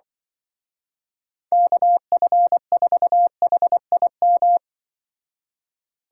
Morse code, heard as a keyed tone, about 23 wpm, 700 Hz. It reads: KF4HIM
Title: Speed Racing - USA Call Signs 20wpm